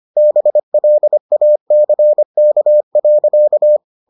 Morse code, heard as a keyed tone, about 25 wpm, 600 Hz. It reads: BLACK.